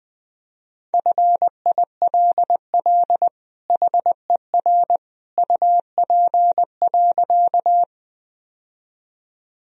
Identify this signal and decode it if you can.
Morse code, heard as a keyed tone, about 20 wpm, 700 Hz. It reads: FILL HER UP.
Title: Sentences from Top 1000 Words - Review Part 3 - 20wpm